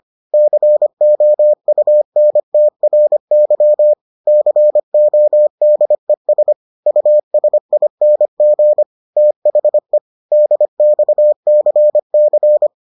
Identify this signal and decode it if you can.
Morse code, heard as a keyed tone, about 25 wpm, 600 Hz. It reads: COUNTRY CODES USING THE DXCC